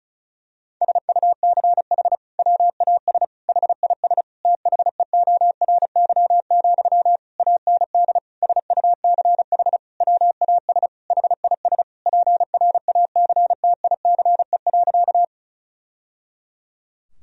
Morse code, heard as a keyed tone, about 36 wpm, 700 Hz. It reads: SUCH WAS HIS THEORY, AND SUCH WAS HIS PRACTICE.
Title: Sentences from Top 1000 Words - Review Part 4 - 35wpm